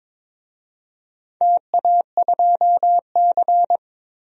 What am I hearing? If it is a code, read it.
Morse code, heard as a keyed tone, about 22 wpm, 700 Hz. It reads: TA2C
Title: DXCC-100 Call Signs - 4 Characters - 22wpm